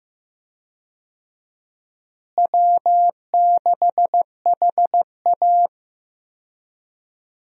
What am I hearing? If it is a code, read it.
Morse code, heard as a keyed tone, about 15 wpm, 700 Hz. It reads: W6HA
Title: USA Call Signs - 4 Characters - Repeated 3x - 15wpm